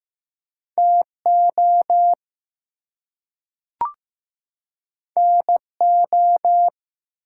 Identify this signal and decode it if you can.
Morse code, heard as a keyed tone, about 15 wpm, 700 Hz. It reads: TO NO
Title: Two Letter Words - 15wpm